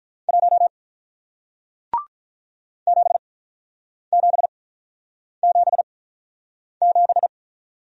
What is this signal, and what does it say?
Morse code, heard as a keyed tone, about 48 wpm, spaced as for 9 wpm, 700 Hz. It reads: 1 7777